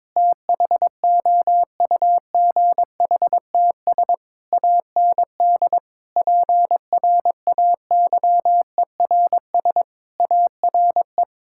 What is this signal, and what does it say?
Morse code, heard as a keyed tone, about 22 wpm, 700 Hz. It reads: THOUGHTS AND PRAYERS ARE
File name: Sentences with 7-Word Phrases - 22wpm